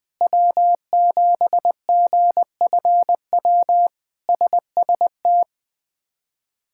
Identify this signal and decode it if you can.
Morse code, heard as a keyed tone, about 20 wpm, 700 Hz. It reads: W7GFW SST
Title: K1USN SST QSO Practice - 20wpm